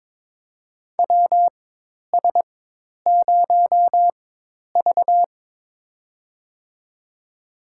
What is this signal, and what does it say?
Morse code, heard as a keyed tone, about 22 wpm, spaced as for 10 wpm, 700 Hz. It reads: WS0V
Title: USA Call Signs - 4 Characters - 4x Character Spacing - 22wpm